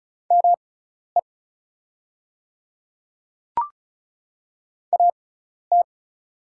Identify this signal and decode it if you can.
Morse code, heard as a keyed tone, about 34 wpm, spaced as for 12 wpm, 690 Hz. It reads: ME AT
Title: 6x Character Spacing - Top 200 Words - 35wpm